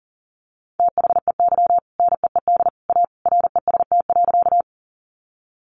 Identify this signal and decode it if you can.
Morse code, heard as a keyed tone, about 40 wpm, 700 Hz. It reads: THEY NEED A REST.
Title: Sentences from Top 500 Words 40wpm